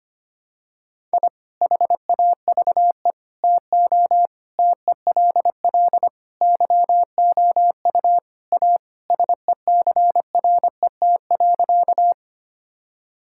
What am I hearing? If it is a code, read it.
Morse code, heard as a keyed tone, about 25 wpm, 700 Hz. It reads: I HAVE TO TELL YOU A SECRET.
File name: practice / 25wpm / Sentences with 3-Word Phrases - 25wpm